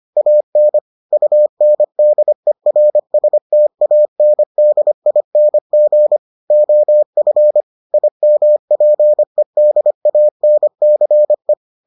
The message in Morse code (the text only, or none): AN UNDERSTANDING OF IMPEDANCE